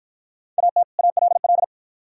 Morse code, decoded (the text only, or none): AT ALL